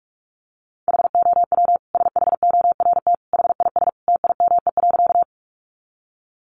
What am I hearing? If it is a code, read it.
Morse code, heard as a keyed tone, about 44 wpm, 700 Hz. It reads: HOW SHORT HIS TIME.
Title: Sentences from Top 400 Words 45wpm